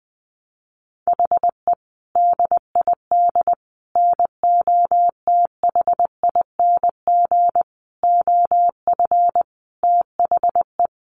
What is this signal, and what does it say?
Morse code, heard as a keyed tone, about 20 wpm, 700 Hz. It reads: HE DID NOTHING OF THE
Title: Sentences from Top 1300 Words 20wpm